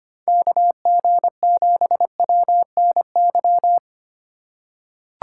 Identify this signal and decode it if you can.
Morse code, heard as a keyed tone, about 25 wpm, 700 Hz. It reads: KG7WNY
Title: USA Call Signs - 6 Characters - 2x Character Spacing - 25wpm